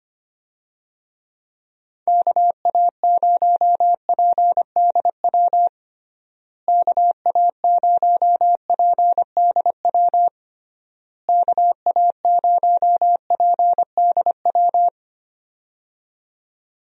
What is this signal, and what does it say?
Morse code, heard as a keyed tone, about 25 wpm, 700 Hz. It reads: KA0PDW KA0PDW KA0PDW
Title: USA Call Signs - 6 Characters - Repeated 3x - 25wpm